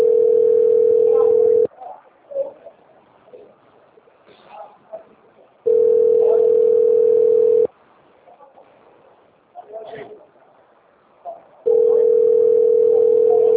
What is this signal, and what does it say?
Ringback tone (North America), 440 + 480 Hz.